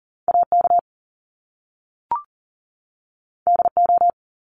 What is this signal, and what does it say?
Morse code, heard as a keyed tone, about 39 wpm, 700 Hz. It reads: AK DO
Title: Top Two Letter Combinations in Words 40wpm